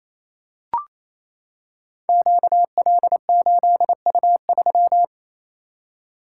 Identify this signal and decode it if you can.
Morse code, heard as a keyed tone, about 28 wpm, 700 Hz. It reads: QL8U3
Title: Code Groups - 5 Letter:Numbers - 28wpm